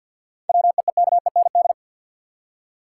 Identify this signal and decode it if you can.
Morse code, heard as a keyed tone, about 53 wpm, 700 Hz. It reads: WEEKEND
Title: Seven Letter Words - 50wpm